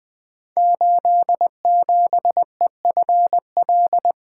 Morse code, heard as a keyed tone, about 20 wpm, 700 Hz. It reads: 87EFL
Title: Code Groups - 5 Letter:Numbers - 20wpm